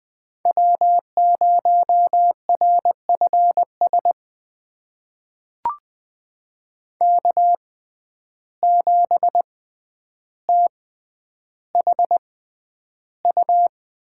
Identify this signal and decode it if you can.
Morse code, heard as a keyed tone, about 20 wpm, spaced as for 10 wpm, 700 Hz. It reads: W0RFS K7THU
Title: USA Call Signs - 5 Characters - 6x Character Spacing - 20wpm